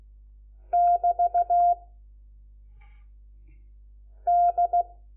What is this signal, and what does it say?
Morse code, heard as a keyed tone, about 16 wpm, 700 Hz. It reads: <BT> D
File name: Lecciones / CLASE Nº4